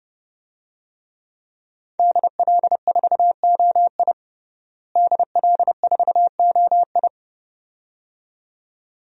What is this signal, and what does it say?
Morse code, heard as a keyed tone, about 31 wpm, 700 Hz. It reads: DL4OI DL4OI
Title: USA Call Signs - 5 Characters - Repeated 2x - 30wpm